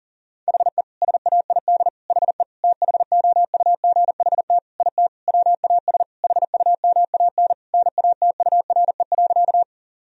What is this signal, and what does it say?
Morse code, heard as a keyed tone, about 40 wpm, 700 Hz. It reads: HE SAID HE THOUGHT IT WAS HUMAN NATURE.
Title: Sentences from Top 1000 Words - Review Part 3 - 40wpm